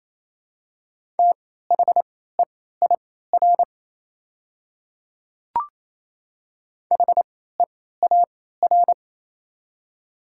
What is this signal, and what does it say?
Morse code, heard as a keyed tone, about 28 wpm, spaced as for 16 wpm, 700 Hz. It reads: THEIR HEAR
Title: 3x Character Spacing - Top 300 Words - 28wpm